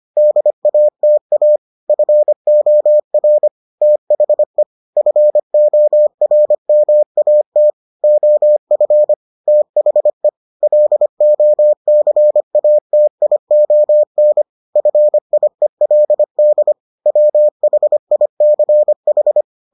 Morse code, heard as a keyed tone, about 25 wpm, 600 Hz. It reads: DATA FOR THE FORMAT OF THE LOCATION FIELD WHICH